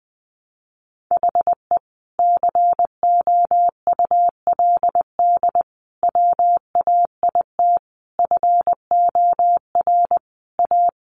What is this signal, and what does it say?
Morse code, heard as a keyed tone, about 20 wpm, 700 Hz. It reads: HE COULD WAIT FOR A
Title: Sentences from Top 1300 Words 20wpm